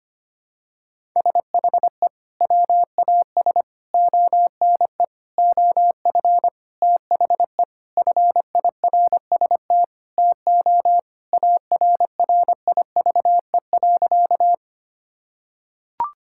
Morse code, heard as a keyed tone, about 25 wpm, 700 Hz. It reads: SHE WAS ONE OF THE FIRST TO ARRIVE.
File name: practice / 25wpm / Sentences with 4-Word Phrases - 25wpm